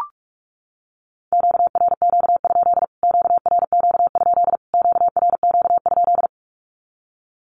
Morse code, heard as a keyed tone, about 44 wpm, 700 Hz. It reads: QRQ? QRQ? QRQ?
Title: Q-Codes Repeated 3x 45wpm